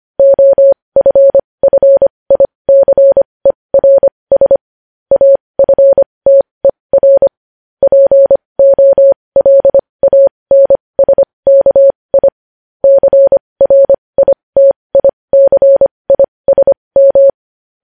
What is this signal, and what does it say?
Morse code, heard as a keyed tone, about 25 wpm, spaced as for 20 wpm, 550 Hz. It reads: OFFICERS AFTER POLANSKI CRITICISM
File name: podcast / News headlines-1x-25x20